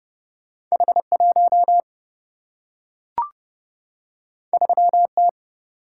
Morse code, heard as a keyed tone, about 30 wpm, 700 Hz. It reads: H 1 3 T
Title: Single-Letter Single-Number 30wpm